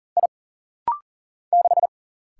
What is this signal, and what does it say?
Morse code, heard as a keyed tone, about 41 wpm, 700 Hz. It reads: I 6
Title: Single Letter-Number - Rapid-Fire 40wpm